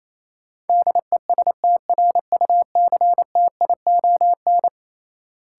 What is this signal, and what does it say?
Morse code, heard as a keyed tone, about 28 wpm, 700 Hz. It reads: DESTRUCTION
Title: Sets of 5 Words Encore - New Word Drill - 28wpm